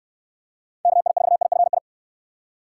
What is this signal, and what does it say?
Morse code, heard as a keyed tone, about 62 wpm, 700 Hz. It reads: KI4IFI